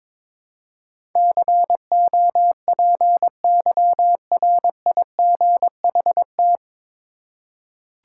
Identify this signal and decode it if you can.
Morse code, heard as a keyed tone, about 22 wpm, 700 Hz. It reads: COPYRIGHT
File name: 9 Letter Words - First 3 Letters Given - 22wpm